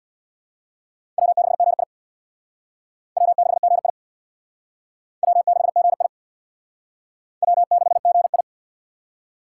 Morse code, heard as a keyed tone, about 63 wpm, 700 Hz. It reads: W6GI W6GI W6GI W6GI